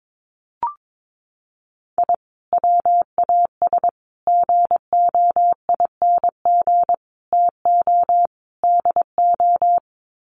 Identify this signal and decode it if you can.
Morse code, heard as a keyed tone, about 22 wpm, 700 Hz. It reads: I WAS GOING TO DO
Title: Sets of 5 Words Encore2 22wpm